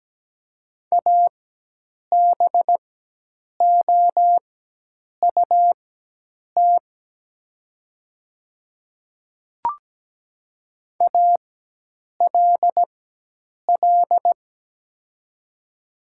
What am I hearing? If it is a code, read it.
Morse code, heard as a keyed tone, about 17 wpm, spaced as for 8 wpm, 700 Hz. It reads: ABOUT ALL
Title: 4x Character Spacing - Top 100 Words - 17wpm